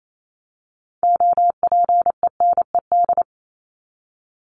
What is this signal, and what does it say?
Morse code, heard as a keyed tone, about 28 wpm, 700 Hz. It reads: OPENED
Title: ED Suffix - Top 100 Words with Base Words 28wpm